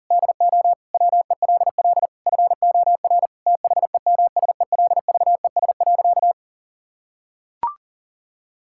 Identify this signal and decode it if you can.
Morse code, heard as a keyed tone, about 37 wpm, 700 Hz. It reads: DO WELL FOR THEMSELVES.
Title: Sentences from Top 1000 Words - Review Part 5 - 40wpm